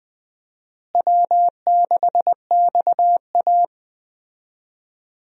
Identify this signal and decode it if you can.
Morse code, heard as a keyed tone, about 20 wpm, 700 Hz. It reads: W6XA
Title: USA Call Signs - 4 Characters - Repeated 3x - 20wpm